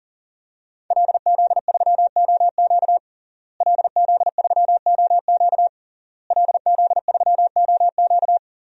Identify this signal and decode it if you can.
Morse code, heard as a keyed tone, about 40 wpm, 700 Hz. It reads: LZ3OQ LZ3OQ LZ3OQ
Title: DXCC-100 Call Signs - 5 Characters - Repeated 3x - 40wpm